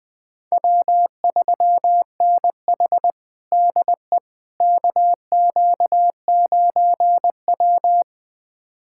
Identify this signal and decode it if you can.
Morse code, heard as a keyed tone, about 21 wpm, 700 Hz. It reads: W3NH DE KQ9W
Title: USA Callsign DE Callsign - 4 Characters - 20wpm